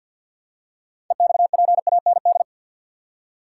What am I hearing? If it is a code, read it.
Morse code, heard as a keyed tone, about 52 wpm, 700 Hz. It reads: EXPAND
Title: Six Letter Words - 50wpm